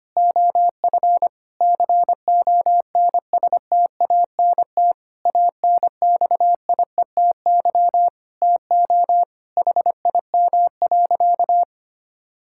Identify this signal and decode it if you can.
Morse code, heard as a keyed tone, about 25 wpm, 700 Hz. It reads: OF CONSTANT ANXIETY TO HIM.
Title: Sentences from Top 1600 Words 25wpm